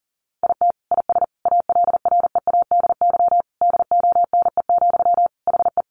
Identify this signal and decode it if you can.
Morse code, heard as a keyed tone, about 40 wpm, 700 Hz. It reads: IT IS ALREADY DONE, HE
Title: Sentences from Top 700 Words 40wpm